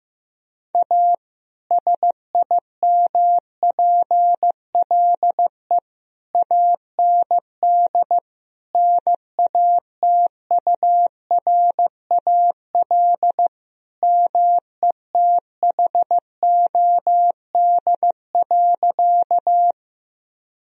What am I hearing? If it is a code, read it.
Morse code, heard as a keyed tone, about 16 wpm, 700 Hz. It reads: A SIMPLE AND NATURAL METHOD.
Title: Sentences from Top 1000 Words - Review Part 4 - 15wpm